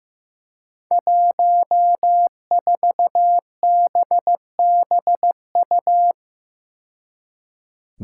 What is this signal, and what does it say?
Morse code, heard as a keyed tone, about 16 wpm, 700 Hz. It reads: 14BBU
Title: Five Challenging Letter-Number Combinations 15wpm